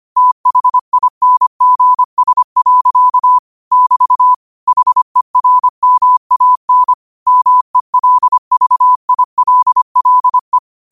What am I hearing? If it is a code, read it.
Morse code, heard as a keyed tone, about 25 wpm, 1000 Hz. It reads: THINGS. <BT> HERMAN MELVILLE